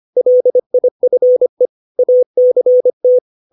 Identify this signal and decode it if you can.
Morse code, heard as a keyed tone, about 25 wpm, 500 Hz. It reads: LIFE ACT
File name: 632 words / 632 Words 25WPM